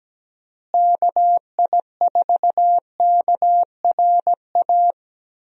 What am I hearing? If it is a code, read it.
Morse code, heard as a keyed tone, about 17 wpm, 700 Hz. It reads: KI4KRA
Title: USA Call Signs - 6 Characters - Repeated 3x - 17wpm